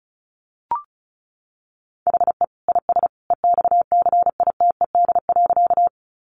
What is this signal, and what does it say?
Morse code, heard as a keyed tone, about 34 wpm, 700 Hz. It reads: HE IS EXCITED.
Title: Additional Easy Phrases 35wpm